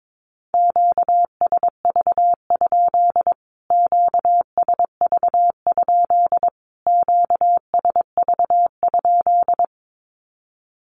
Morse code, heard as a keyed tone, about 22 wpm, 700 Hz. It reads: QSV? QSV? QSV?
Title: Q-Codes Repeated 3x 22wpm